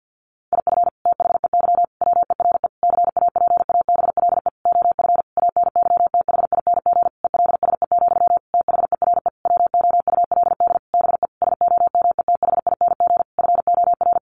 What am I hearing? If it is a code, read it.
Morse code, heard as a keyed tone, about 48 wpm, 700 Hz. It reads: IF THEY WERE CAPABLE OF ANYTHING ELSE, THERE WOULD BE SOMETHING FOR
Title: Sentences from Top 1200 Words 50wpm